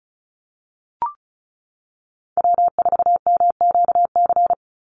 Morse code, heard as a keyed tone, about 34 wpm, 700 Hz. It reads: W4MQC
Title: Call Signs - 35wpm